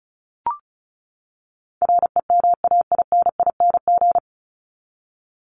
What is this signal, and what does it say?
Morse code, heard as a keyed tone, about 34 wpm, 700 Hz. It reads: REMAINING
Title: ING Suffix - Top 100 Words with Base Words 35wpm